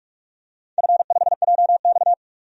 Morse code, heard as a keyed tone, about 46 wpm, 700 Hz. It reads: F5JX
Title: DXCC-100 Call Signs - 4 Characters - 45wpm